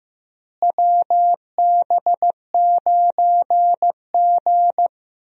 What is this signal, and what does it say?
Morse code, heard as a keyed tone, about 16 wpm, 700 Hz. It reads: WB9G
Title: USA Call Signs - 4 Characters - Repeated 2x - 15wpm